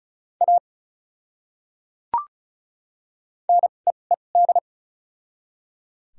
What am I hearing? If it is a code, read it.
Morse code, heard as a keyed tone, about 36 wpm, spaced as for 25 wpm, 700 Hz. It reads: A NEED
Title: 2x Character Spacing - Top 500 Words - 35wpm